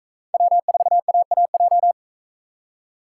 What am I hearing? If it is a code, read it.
Morse code, heard as a keyed tone, about 43 wpm, 700 Hz. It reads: W4AAJ